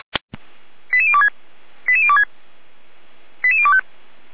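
Five-tone selective call (ZVEI), heard as quick runs of five tones, tones of about 80 ms. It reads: 88026 88026 88025
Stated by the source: Five-tone ZVEI selective call received by SDR, from audio server of feuerwehr-orlamuende.org fire brigade.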